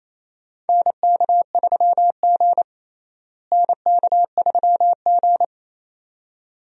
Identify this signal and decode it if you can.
Morse code, heard as a keyed tone, about 28 wpm, 700 Hz. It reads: NK3G NK3G
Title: USA Call Signs - 4 Characters - Repeated 2x - 28wpm